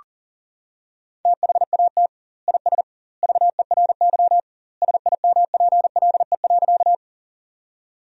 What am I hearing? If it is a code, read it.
Morse code, heard as a keyed tone, about 40 wpm, 700 Hz. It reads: THAT IS VERY SIMPLE.
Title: 2x Word-Spacing - Sentences from Top 400 Words - 40wpm